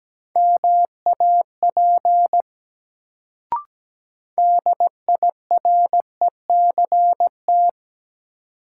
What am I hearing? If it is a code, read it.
Morse code, heard as a keyed tone, about 17 wpm, 700 Hz. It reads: MAP DIRECT
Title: Sentences from Top 400 Words - New Word Drill - 17wpm